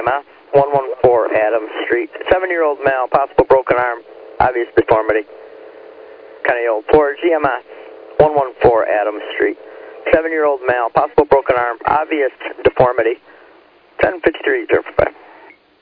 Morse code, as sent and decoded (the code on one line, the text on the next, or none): none